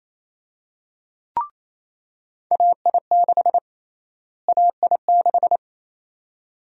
Morse code, heard as a keyed tone, about 27 wpm, 700 Hz. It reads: AI6 AI6
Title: USA Call Sign Prefixes - Repeated 2x - 28wpm